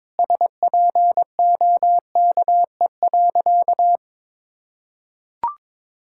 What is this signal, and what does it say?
Morse code, heard as a keyed tone, about 22 wpm, 700 Hz. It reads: SPOKE.
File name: practice / 22wpm / Sentences with 3-Word Phrases - 22wpm